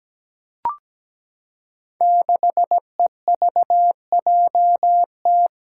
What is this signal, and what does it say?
Morse code, heard as a keyed tone, about 17 wpm, 700 Hz. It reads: 6EVJT
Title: Five Challenging Letter-Number Combinations 17wpm